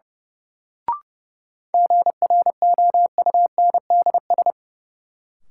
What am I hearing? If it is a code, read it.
Morse code, heard as a keyed tone, about 30 wpm, 700 Hz. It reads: GROUNDS
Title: Sentences from Top 1300 Words - New Word Drill Alt - 30wpm